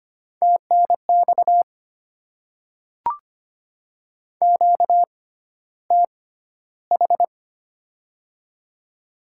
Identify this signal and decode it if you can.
Morse code, heard as a keyed tone, about 25 wpm, spaced as for 13 wpm, 700 Hz. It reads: TNX QTH
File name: Top 20 QSO Elements - 6x Character Spacing - 25wpm